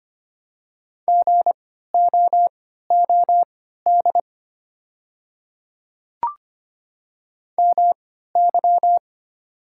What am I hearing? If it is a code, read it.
Morse code, heard as a keyed tone, about 25 wpm, spaced as for 14 wpm, 700 Hz. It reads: GOOD MY
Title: 3x Character Spacing - Top 500 Words - 25wpm